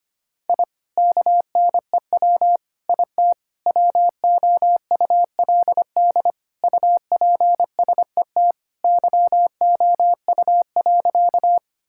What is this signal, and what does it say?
Morse code, heard as a keyed tone, about 25 wpm, 700 Hz. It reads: I KNEW IT WOULD UPSET YOU.
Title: Sentences from Top 1000 Words - Review Part 5 - 25wpm